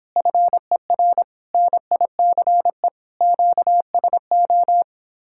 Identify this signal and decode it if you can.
Morse code, heard as a keyed tone, about 26 wpm, 700 Hz. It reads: FER NICE QSO